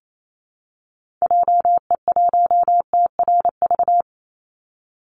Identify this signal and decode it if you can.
Morse code, heard as a keyed tone, about 28 wpm, 700 Hz. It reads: JE1TRV
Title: Call Signs Repeated 2x - 28wpm